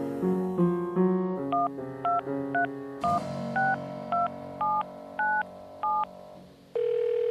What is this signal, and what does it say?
DTMF, heard as telephone touch tones, tones of about 160 ms. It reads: 123162797